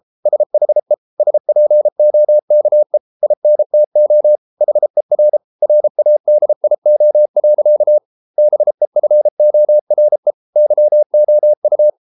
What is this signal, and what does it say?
Morse code, heard as a keyed tone, about 33 wpm, 600 Hz. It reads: SHE SPOKE INTO HER RADIO. BEFORE YOU